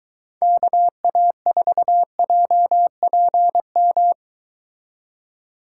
Morse code, heard as a keyed tone, about 22 wpm, 700 Hz. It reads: KA4JPM